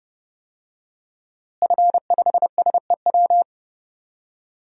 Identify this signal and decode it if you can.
Morse code, heard as a keyed tone, about 30 wpm, 700 Hz. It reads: F5SEW
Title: DXCC-100 Call Signs - 5 Characters - Repeated 2x - 30wpm